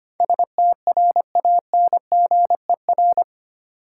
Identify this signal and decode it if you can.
Morse code, heard as a keyed tone, about 25 wpm, 700 Hz. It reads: STRANGER
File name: Eight Letter Words - 25wpm